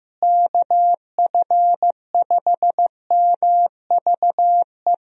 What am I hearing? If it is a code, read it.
Morse code, heard as a keyed tone, about 15 wpm, 700 Hz. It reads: KF5MVE
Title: USA Call Signs - 6 Characters - Repeated 3x - 15wpm